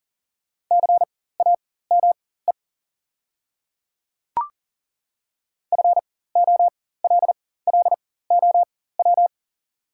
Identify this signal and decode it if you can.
Morse code, heard as a keyed tone, about 40 wpm, spaced as for 19 wpm, 700 Hz. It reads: CAME FOLLOW
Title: 4x Character Spacing - Top 200 Words - 40wpm